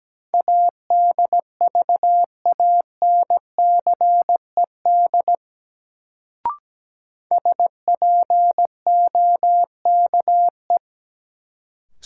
Morse code, heard as a keyed tone, about 17 wpm, 700 Hz. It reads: ADVANCED SPOKE
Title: Sentences from Top 1200 Words - New Word Drill - 17wpm